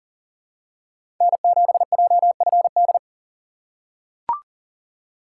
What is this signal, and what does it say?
Morse code, heard as a keyed tone, about 40 wpm, 700 Hz. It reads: N7JFD
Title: USA Call Signs - 5 Characters - 4x Character Spacing - 40wpm